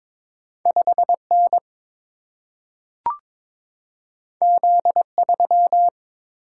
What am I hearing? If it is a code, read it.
Morse code, heard as a keyed tone, about 22 wpm, 700 Hz. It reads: 5 N Z 3
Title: Single-Letter Single-Number 22wpm